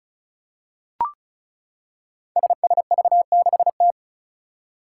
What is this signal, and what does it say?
Morse code, heard as a keyed tone, about 34 wpm, 700 Hz. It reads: SSV6T
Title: Five Challenging Letter-Number Combinations 35wpm